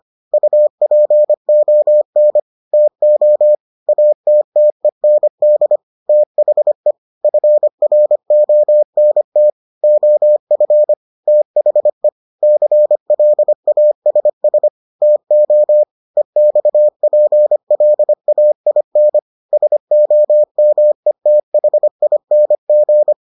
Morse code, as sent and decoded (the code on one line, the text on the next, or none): ..- .--. --- -. / - --- / .- - - . -. -.. / - .... / ..-. .-. --- -. - / --- ..-. / - .... / -.-. .-.. .- ... ... / - --- / . -..- .--. .-.. .- .. -. / ... --- -- . - .... .. -. --.
UPON TO ATTEND THE FRONT OF THE CLASS TO EXPLAIN SOMETHING